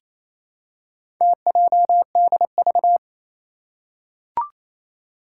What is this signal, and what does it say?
Morse code, heard as a keyed tone, about 27 wpm, 700 Hz. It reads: TJDV
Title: Four Challenging Letter-Number Combinations 28wpm